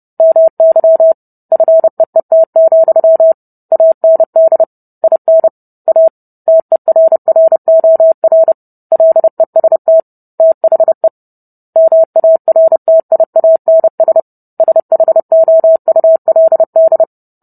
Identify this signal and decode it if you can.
Morse code, heard as a keyed tone, about 30 wpm, 650 Hz. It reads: MY FEET, AND IN A TERROR LEST THE MARTIANS SHOULD